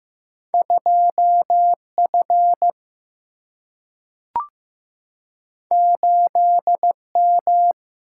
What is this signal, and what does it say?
Morse code, heard as a keyed tone, about 15 wpm, 700 Hz. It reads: 2 F 8 M
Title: Single-Letter Single-Number 15wpm